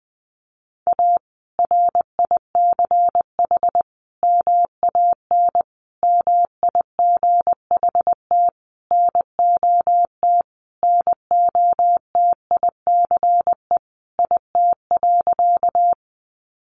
Morse code, heard as a keyed tone, about 19 wpm, 700 Hz. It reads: A RICH MAN MIGHT NOT NOTICE IT.
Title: Sentences from Top 1100 Words 20wpm